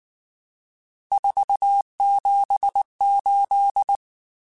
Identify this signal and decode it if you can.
Morse code, heard as a keyed tone, about 19 wpm, 800 Hz. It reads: 478
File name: RSTs / RST-mp3s / Various RST @18